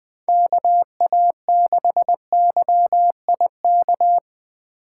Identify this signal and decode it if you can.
Morse code, heard as a keyed tone, about 24 wpm, 700 Hz. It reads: KA6YIK
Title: Speed Racing - USA 2x3 Call Signs 20wpm